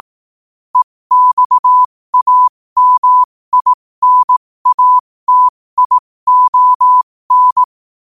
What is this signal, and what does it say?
Morse code, heard as a keyed tone, about 18 wpm, spaced as for 15 wpm, 1000 Hz. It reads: EXAMINATION
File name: qotdfiles / 15wpm